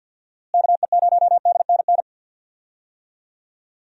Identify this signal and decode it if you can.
Morse code, heard as a keyed tone, about 52 wpm, 700 Hz. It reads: KE0DNN
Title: USA Call Signs - 6 Characters - Repeated 2x - 50wpm